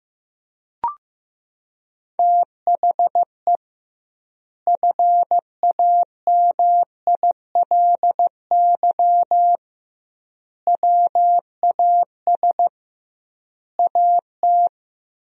Morse code, heard as a keyed tone, about 16 wpm, 700 Hz. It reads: THE FAMILY WAS AT
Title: 2x Word-Spacing - Sentences from Top 400 Words - 15wpm